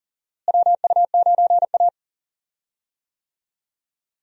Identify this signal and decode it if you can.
Morse code, heard as a keyed tone, about 40 wpm, 700 Hz. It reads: WU9A
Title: USA Call Signs - 4 Characters - Repeated 2x - 40wpm